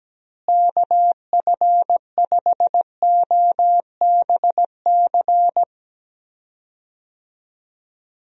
Morse code, heard as a keyed tone, about 17 wpm, 700 Hz. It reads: KF5OBC
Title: USA Call Signs - 6 Characters - Repeated 2x - 17wpm